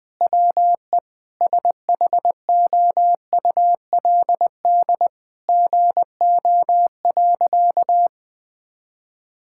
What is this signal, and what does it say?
Morse code, heard as a keyed tone, about 21 wpm, 700 Hz. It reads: WE SHOULD GO.
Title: Sentences with 5-Word Phrases - 20wpm